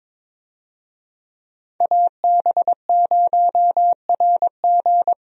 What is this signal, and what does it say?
Morse code, heard as a keyed tone, about 22 wpm, 700 Hz. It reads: AB0RG
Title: USA Call Signs - 5 Characters - 22wpm